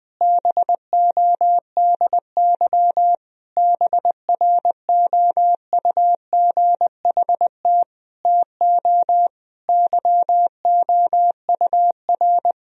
Morse code, heard as a keyed tone, about 21 wpm, 700 Hz. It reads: BODY BROUGHT TO YOUR
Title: Sentences from Top 1000 Words - Review Part 2 - 20wpm